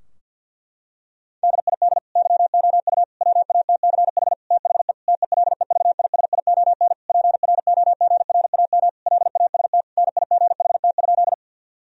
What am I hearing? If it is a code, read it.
Morse code, heard as a keyed tone, about 51 wpm, 700 Hz. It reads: DID YOU WATCH THE TELEVISION PROGRAM LAST NIGHT?
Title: Sentences from Top 1000 Words - Review Part 4 - 50wpm